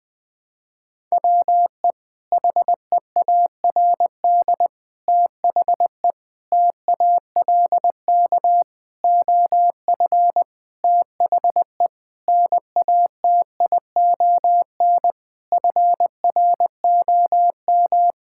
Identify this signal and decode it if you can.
Morse code, heard as a keyed tone, about 20 wpm, 700 Hz. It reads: WE HEARD THE TALK OF THE NATION FROM
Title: Sentences with 5-Word Phrases - 20wpm